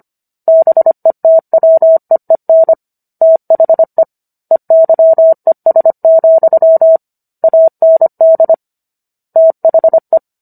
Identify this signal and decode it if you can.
Morse code, heard as a keyed tone, about 25 wpm, 650 Hz. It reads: BETWEEN THE EYES, AND THE